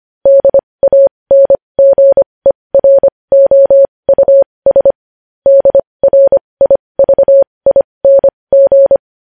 Morse code, heard as a keyed tone, about 25 wpm, spaced as for 20 wpm, 550 Hz. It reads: DANGEROUS DRIVING
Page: news-headlines-in-morse-code-at-20-wpm